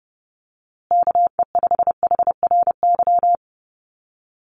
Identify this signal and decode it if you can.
Morse code, heard as a keyed tone, about 30 wpm, 700 Hz. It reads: KE5HRY
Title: USA Call Signs Repeated 2x 30wpm